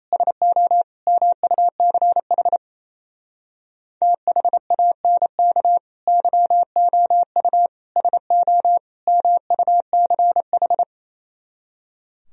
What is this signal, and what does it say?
Morse code, heard as a keyed tone, about 35 wpm, 700 Hz. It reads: SO MUCH THANK YOU SO MUCH
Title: Speed-Racing - Sets of 4 Words 28wpm